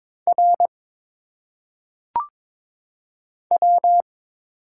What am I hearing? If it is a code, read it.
Morse code, heard as a keyed tone, about 22 wpm, 700 Hz. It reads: R W
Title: Single Letters - Phonetic Pronunciation 22wpm